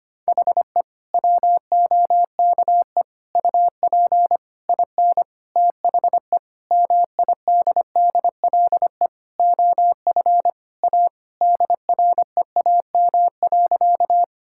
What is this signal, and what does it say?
Morse code, heard as a keyed tone, about 25 wpm, 700 Hz. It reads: HE WOKE UP IN THE MIDDLE OF A DREAM.
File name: practice / 25wpm / Sentences with 4-Word Phrases - 25wpm